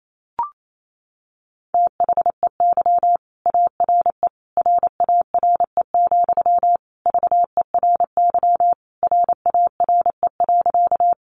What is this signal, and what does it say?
Morse code, heard as a keyed tone, about 28 wpm, 700 Hz. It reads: THEY ARE RARE, VERY RARE.
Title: Sentences from Top 1100 Words 28wpm